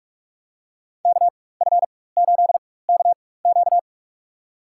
Morse code, heard as a keyed tone, about 46 wpm, spaced as for 21 wpm, 700 Hz. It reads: KF8KQ
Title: USA Call Signs - 5 Characters - 4x Character Spacing - 45wpm